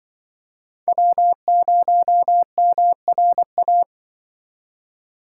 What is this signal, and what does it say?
Morse code, heard as a keyed tone, about 23 wpm, 700 Hz. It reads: W0MRA
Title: Speed Racing - USA Call Signs 20wpm